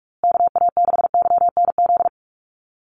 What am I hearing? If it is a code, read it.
Morse code, heard as a keyed tone, about 44 wpm, 700 Hz. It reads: KA6YNZ
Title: USA Call Signs Repeated 2x 45wpm